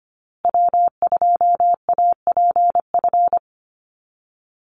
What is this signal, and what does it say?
Morse code, heard as a keyed tone, about 25 wpm, 700 Hz. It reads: W2APF
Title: Call Signs Repeated 2x - 25wpm